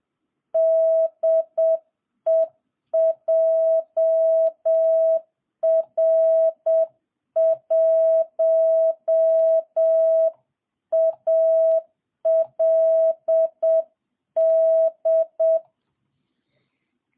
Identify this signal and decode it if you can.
Morse code, heard as a keyed tone, about 7 wpm, 660 Hz. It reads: DE JR1ALD